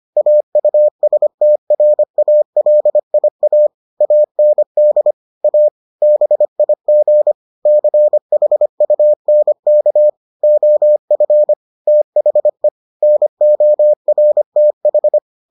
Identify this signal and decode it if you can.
Morse code, heard as a keyed tone, about 25 wpm, 600 Hz. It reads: AUSTRALIA AND A BIG CHUNK OF THE NORTH